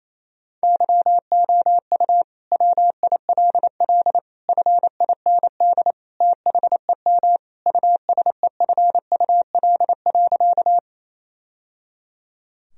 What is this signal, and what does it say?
Morse code, heard as a keyed tone, about 28 wpm, 700 Hz. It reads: YOU WILL FIND THEM USEFUL.
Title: Sentences from Top 1000 Words - Review Part 4 - 28wpm